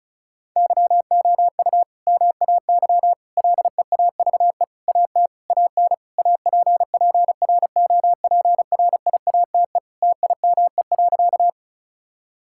Morse code, heard as a keyed tone, about 34 wpm, 700 Hz. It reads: YOU MAY LEAVE AT AN APPROPRIATE TIME.
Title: Sentences from Top 1000 Words - Review Part 5 - 35wpm